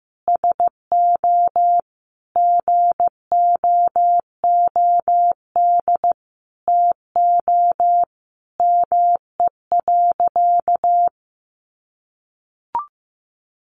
Morse code, heard as a keyed tone, about 15 wpm, 700 Hz. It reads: SO GOOD TO ME.
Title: Sentences from Top 1400 Words 15wpm